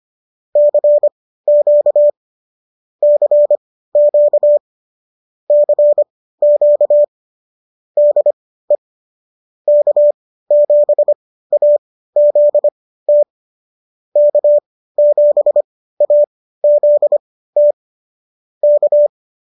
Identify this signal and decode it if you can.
Morse code, heard as a keyed tone, about 25 wpm, spaced as for 15 wpm, 590 Hz. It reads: CQ CQ CQ DE K7AZT K7AZT K